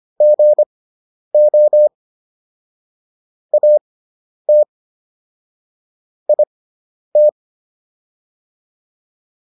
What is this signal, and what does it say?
Morse code, heard as a keyed tone, about 25 wpm, spaced as for 10 wpm, 600 Hz. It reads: GO AT IT